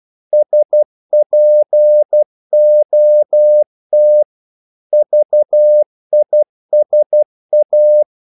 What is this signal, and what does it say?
Morse code, heard as a keyed tone, about 12 wpm, 600 Hz. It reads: SPOT VISA
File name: Korte ord DK 12wpm